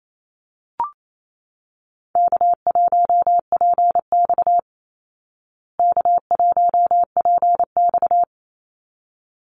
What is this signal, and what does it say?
Morse code, heard as a keyed tone, about 28 wpm, 700 Hz. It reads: K1PX K1PX
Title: Call Signs Repeated 3x - 28wpm